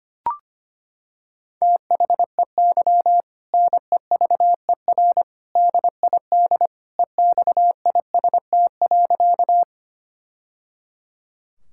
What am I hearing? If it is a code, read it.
Morse code, heard as a keyed tone, about 25 wpm, 700 Hz. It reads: THEY NEVER DID EXIST.
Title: Sentences from Top 1000 Words - Review Part 5 - 25wpm